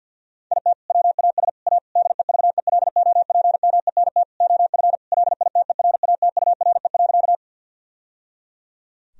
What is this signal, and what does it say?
Morse code, heard as a keyed tone, about 50 wpm, 700 Hz. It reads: IT WAS A DEVELOPMENT OF LITERATURE.
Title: Sentences from Top 1000 Words - Review Part 4 - 50wpm